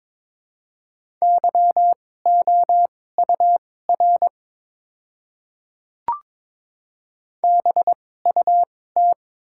Morse code, heard as a keyed tone, about 22 wpm, spaced as for 16 wpm, 690 Hz. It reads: YOUR BUT
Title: 2x Character Spacing - Top 100 Words - 22wpm